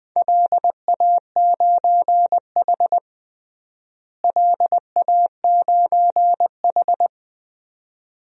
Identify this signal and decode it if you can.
Morse code, heard as a keyed tone, about 21 wpm, 700 Hz. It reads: LA9H LA9H
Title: DXCC-100 Call Signs - 4 Characters - Repeated 2x - 20wpm